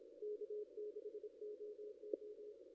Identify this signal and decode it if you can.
Morse code, heard as a keyed tone, about 28 wpm, 440 Hz. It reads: K6OO
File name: contests / ss-cw-k5tr-op-n5rz -2200